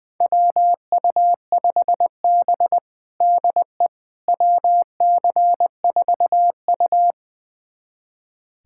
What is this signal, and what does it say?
Morse code, heard as a keyed tone, about 21 wpm, 700 Hz. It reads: WU5B DE WC4U
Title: USA Callsign DE Callsign - 4 Characters - 20wpm